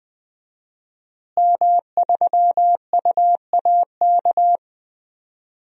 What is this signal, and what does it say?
Morse code, heard as a keyed tone, about 20 wpm, 700 Hz. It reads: M3UAK
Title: DXCC-100 Call Signs - 5 Characters - Repeated 2x - 20wpm